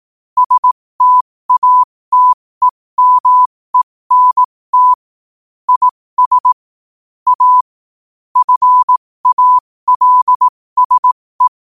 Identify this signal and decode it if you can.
Morse code, heard as a keyed tone, about 18 wpm, spaced as for 15 wpm, 1000 Hz. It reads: STATEMENT IS A FALSE
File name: qotdfiles / 15wpm